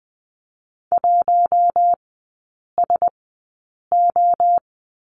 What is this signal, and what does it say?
Morse code, heard as a keyed tone, about 20 wpm, spaced as for 8 wpm, 690 Hz. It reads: 1SO
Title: 081 Lesson 14e - Number 2 - Three Characters 20wpm